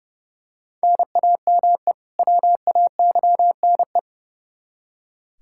Morse code, heard as a keyed tone, about 30 wpm, 700 Hz. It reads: NAME WAYNE
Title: Practice QSO - Exchange 1 - Beginning - 30wpm